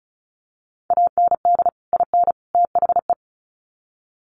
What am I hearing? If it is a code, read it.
Morse code, heard as a keyed tone, about 34 wpm, 700 Hz. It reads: AND IN THE
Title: Sets of 3 Words 35wpm